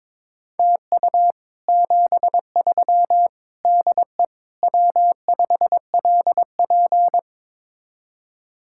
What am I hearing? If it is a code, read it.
Morse code, heard as a keyed tone, about 23 wpm, 700 Hz. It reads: TU 73 DE W5LP
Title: Speed-Racing Common QSO Phrases Encore - 22wpm